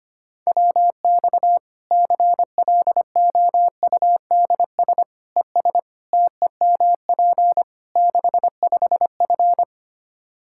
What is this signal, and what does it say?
Morse code, heard as a keyed tone, about 25 wpm, 700 Hz. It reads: WX CLOUDS ES TEMP 65F
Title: Practice QSO - Exchange 2 - Info - 25wpm